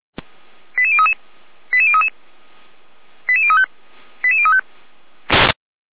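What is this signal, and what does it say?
Five-tone selective call (ZVEI), heard as quick runs of five tones, tones of about 70 ms. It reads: 88033 88033 88035 88035
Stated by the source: ZVEI five-tone selective call received on SDR.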